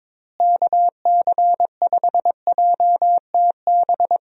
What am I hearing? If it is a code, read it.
Morse code, heard as a keyed tone, about 22 wpm, 700 Hz. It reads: KC5JTB
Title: USA Call Signs - 6 Characters - Repeated 3x - 22wpm